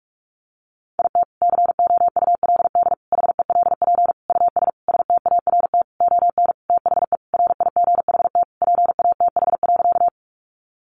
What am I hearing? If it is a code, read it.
Morse code, heard as a keyed tone, about 44 wpm, 700 Hz. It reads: IT COULD HELP US START ON THE RIGHT PATH.
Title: Sentences from Top 1400 Words 45wpm